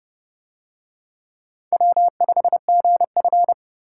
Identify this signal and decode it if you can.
Morse code, heard as a keyed tone, about 29 wpm, 700 Hz. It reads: W5GF